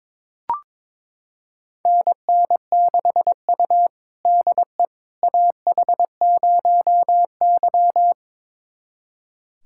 Morse code, heard as a keyed tone, about 22 wpm, 700 Hz. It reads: NN6U DE AH0Y
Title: USA Callsign DE Callsign - 4 Characters - 22wpm